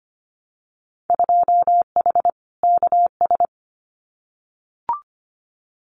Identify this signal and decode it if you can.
Morse code, heard as a keyed tone, about 25 wpm, 700 Hz. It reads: 2H KS
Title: Winter Field Day Exchanges Encore 25wpm